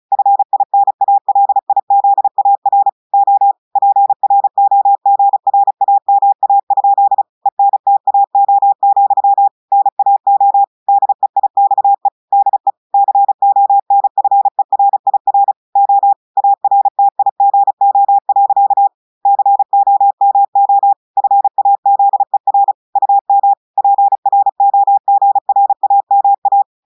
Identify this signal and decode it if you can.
Morse code, heard as a keyed tone, about 33 wpm, 800 Hz. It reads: FINALIZAR O PROGRAMA? ENTAO, NAO DEIXE DE CONFERIR O ARTIGO. COMO FAZER UM PROGRAMA
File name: audios / portugais / portugais texte en clair 2 vitesse 35 mots minutes